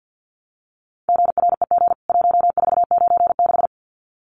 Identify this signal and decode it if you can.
Morse code, heard as a keyed tone, about 48 wpm, 700 Hz. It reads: GREG 1496
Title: CWT Exchange 50wpm